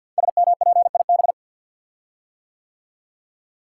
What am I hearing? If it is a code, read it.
Morse code, heard as a keyed tone, about 50 wpm, 700 Hz. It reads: SMPIB
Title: Code Groups - 5 Letters - 50wpm